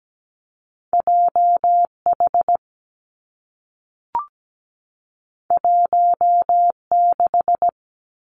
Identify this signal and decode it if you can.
Morse code, heard as a keyed tone, about 17 wpm, 700 Hz. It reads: JH 16
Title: Two Challenging Letter-Number Combinations 17wpm